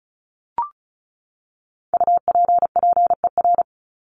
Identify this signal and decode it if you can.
Morse code, heard as a keyed tone, about 34 wpm, 700 Hz. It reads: UPPER
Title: Top 1751-2000 Words 35wpm